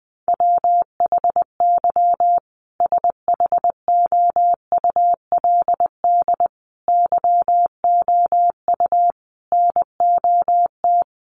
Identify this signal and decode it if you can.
Morse code, heard as a keyed tone, about 20 wpm, 700 Hz. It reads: WHY SHOULD YOU NOT
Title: Sentences from Top 1100 Words 20wpm